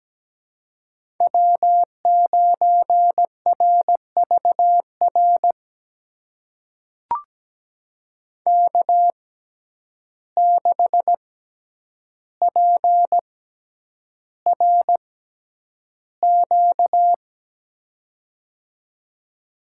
Morse code, heard as a keyed tone, about 17 wpm, spaced as for 9 wpm, 700 Hz. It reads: W9RVR K6PRQ
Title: USA Call Signs - 5 Characters - 6x Character Spacing - 17wpm